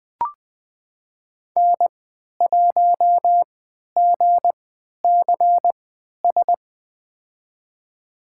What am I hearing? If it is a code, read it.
Morse code, heard as a keyed tone, about 20 wpm, spaced as for 11 wpm, 700 Hz. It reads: N1GCS
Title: USA Call Signs - 5 Characters - 3x Character Spacing - 20wpm